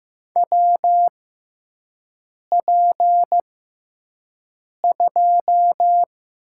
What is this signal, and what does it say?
Morse code, heard as a keyed tone, about 16 wpm, spaced as for 5 wpm, 700 Hz. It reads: WP2
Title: USA Call Signs - 6 Characters - 6x Character Spacing - 15wpm